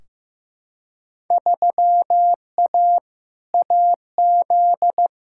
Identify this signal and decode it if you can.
Morse code, heard as a keyed tone, about 15 wpm, 700 Hz. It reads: 3A AZ
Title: Field Day Exchanges - 15wpm